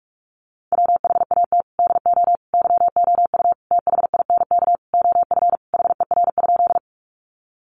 Morse code, heard as a keyed tone, about 44 wpm, 700 Hz. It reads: WHAT DO YOU THINK OF HER?
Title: Sentences from Top 200 Words 45wpm